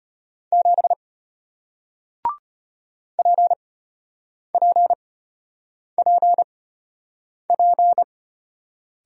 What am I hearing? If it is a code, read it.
Morse code, heard as a keyed tone, about 33 wpm, spaced as for 8 wpm, 700 Hz. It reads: 7 PPPP